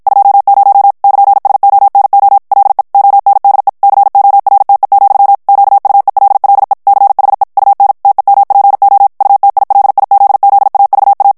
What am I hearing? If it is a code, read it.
Morse code, heard as a keyed tone, about 52 wpm, 800 Hz. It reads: 20 <KN>SONO LE ONDE CORTE, QUELLE CHE UN TEMPO UTILIZZAVA
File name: morse mp3 / ITALIEN texte clair 1 vitesse 50 mots minute